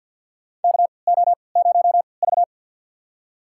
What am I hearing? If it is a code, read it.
Morse code, heard as a keyed tone, about 51 wpm, spaced as for 28 wpm, 700 Hz. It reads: KO0V
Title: USA Call Signs - 4 Characters - 3x Character Spacing - 50wpm